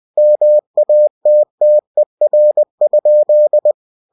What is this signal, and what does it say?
Morse code, heard as a keyed tone, about 20 wpm, 600 Hz. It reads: MATTER?
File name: cw-books / winnie-the-pooh / winnie-20wpm-page-0022